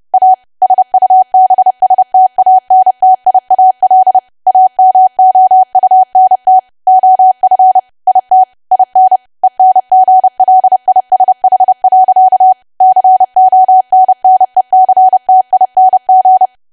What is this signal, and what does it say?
Morse code, heard as a keyed tone, about 30 wpm, 750 Hz. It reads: A SUBSTANTIAL AMOUNT OF IT IN ENGLISH. CONNECTING
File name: cwa / W1AW-Level3 / 30WPM